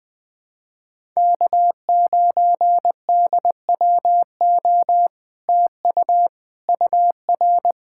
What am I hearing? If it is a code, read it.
Morse code, heard as a keyed tone, about 20 wpm, 700 Hz. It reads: K9DWO TU UR
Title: Speed-Racing Common QSO Phrases - 17wpm